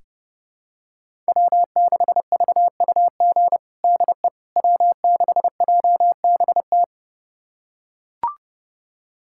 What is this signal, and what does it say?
Morse code, heard as a keyed tone, about 30 wpm, 700 Hz. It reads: W6VUG DE W6JBT
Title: USA Callsign DE Callsign - 5 Characters - 30wpm